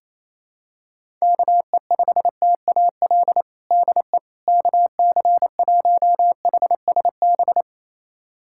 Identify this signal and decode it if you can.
Morse code, heard as a keyed tone, about 28 wpm, 700 Hz. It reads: KE5TAL DE KC1HSB
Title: USA Callsign DE Callsign - 6 Characters - 28wpm